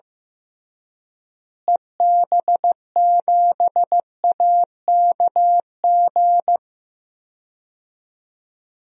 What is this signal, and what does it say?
Morse code, heard as a keyed tone, about 15 wpm, 700 Hz. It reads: EB7AKG
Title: DXCC-100 Call Signs - 6 Characters - Repeated 3x - 15wpm